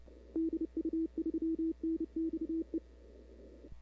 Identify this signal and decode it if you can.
Morse code, heard as a keyed tone, about 30 wpm, 320 Hz. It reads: DU3NXE
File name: CW / 2005-k5tr arrl cw dx -00:00